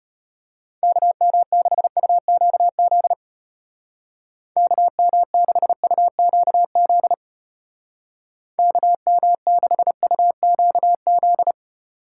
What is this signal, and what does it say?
Morse code, heard as a keyed tone, about 33 wpm, 700 Hz. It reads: KM6UQZ KM6UQZ KM6UQZ